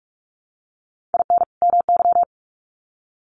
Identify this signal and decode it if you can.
Morse code, heard as a keyed tone, about 44 wpm, 700 Hz. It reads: IN MY
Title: Sets of 2 Words 45wpm